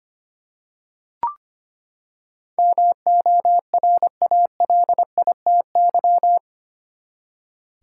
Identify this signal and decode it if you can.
Morse code, heard as a keyed tone, about 25 wpm, 700 Hz. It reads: MORALITY
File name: Eight Letter Words - 25wpm